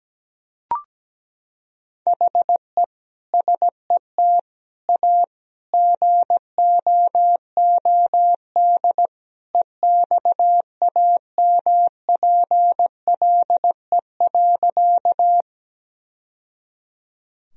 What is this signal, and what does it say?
Morse code, heard as a keyed tone, about 17 wpm, 700 Hz. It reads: HE SET A GOOD EXAMPLE.
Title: Sentences from Top 1000 Words - Review Part 1 - 17wpm